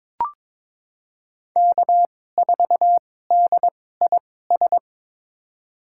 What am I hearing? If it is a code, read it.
Morse code, heard as a keyed tone, about 22 wpm, spaced as for 16 wpm, 700 Hz. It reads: K4DIS
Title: USA Call Signs - 5 Characters - 2x Character Spacing - 22wpm